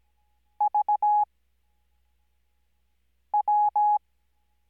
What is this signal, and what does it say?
Morse code, heard as a keyed tone, about 17 wpm, spaced as for 4 wpm, 840 Hz. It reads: VW
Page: Farnsworth Spacing: 15WPM Character speed Spaced at 5WPM.